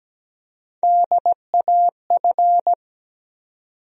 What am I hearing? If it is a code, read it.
Morse code, heard as a keyed tone, about 17 wpm, 700 Hz. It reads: DAF
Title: USA Call Sign Postfixes - 3 Letters 17wpm